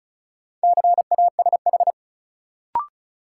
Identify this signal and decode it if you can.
Morse code, heard as a keyed tone, about 36 wpm, 700 Hz. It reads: CASH
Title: Sentences from Top 700 Words - New Word Drill - 35wpm